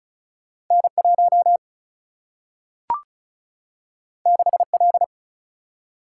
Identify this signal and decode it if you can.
Morse code, heard as a keyed tone, about 36 wpm, 700 Hz. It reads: N 1 6 L
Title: Single-Letter Single-Number 35wpm